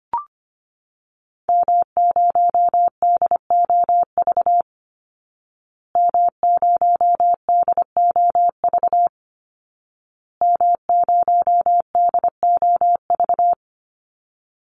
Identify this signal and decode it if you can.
Morse code, heard as a keyed tone, about 25 wpm, 700 Hz. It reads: M0DOV M0DOV M0DOV
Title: Call Signs Repeated 2x - 25wpm